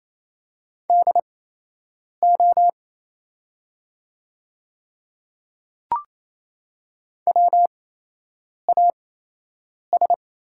Morse code, heard as a keyed tone, about 27 wpm, spaced as for 8 wpm, 700 Hz. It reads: DO WAS
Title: 8x Character Spacing - Top 200 Words - 28wpm